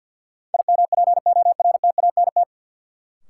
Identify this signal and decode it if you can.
Morse code, heard as a keyed tone, about 53 wpm, 700 Hz. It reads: IMPORTANT
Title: Sets of 3 Words - New Word Drill - 50wpm